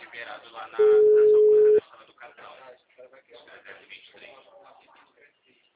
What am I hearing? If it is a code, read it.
Ringback tone (Europe, 425 Hz), with speech.